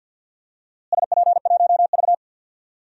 Morse code, heard as a keyed tone, about 50 wpm, 700 Hz. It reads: SP1V
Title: DXCC-100 Call Signs - 4 Characters - Repeated 3x - 50wpm